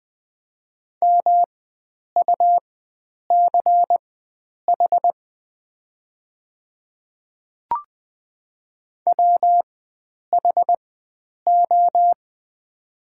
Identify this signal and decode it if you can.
Morse code, heard as a keyed tone, about 20 wpm, spaced as for 9 wpm, 690 Hz. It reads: MUCH WHO
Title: 4x Character Spacing - Top 300 Words - 20wpm